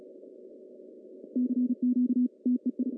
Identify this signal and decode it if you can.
Morse code, heard as a keyed tone, about 36 wpm, 250 Hz. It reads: CQ TEI
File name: CW / cqww-cw-k5tr